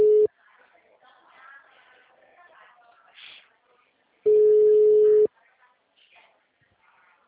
Ringback tone (Europe, 425 Hz).